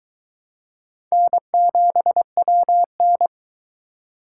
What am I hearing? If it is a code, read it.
Morse code, heard as a keyed tone, about 22 wpm, 700 Hz. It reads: N7WN